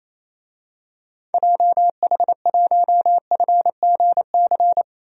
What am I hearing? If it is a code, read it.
Morse code, heard as a keyed tone, about 28 wpm, 700 Hz. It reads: JH1FGC
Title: DXCC-100 Call Signs - 6 Characters - Repeated 3x - 28wpm